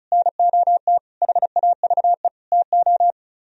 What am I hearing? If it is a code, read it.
Morse code, heard as a keyed tone, about 44 wpm, 700 Hz. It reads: NOT HAVE TO